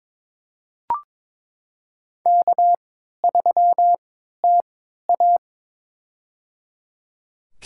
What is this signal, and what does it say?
Morse code, heard as a keyed tone, about 22 wpm, spaced as for 13 wpm, 700 Hz. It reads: K3TA
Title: USA Call Signs - 3x Character Spacing - 22wpm